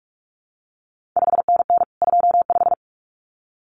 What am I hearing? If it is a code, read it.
Morse code, heard as a keyed tone, about 46 wpm, 700 Hz. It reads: 5NN 25
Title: CQ WW DX Contest Exchange - 45wpm